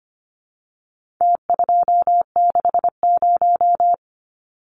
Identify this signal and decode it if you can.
Morse code, heard as a keyed tone, about 25 wpm, 700 Hz. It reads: T260
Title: Numbers - Contest Style - 4-Digit 25wpm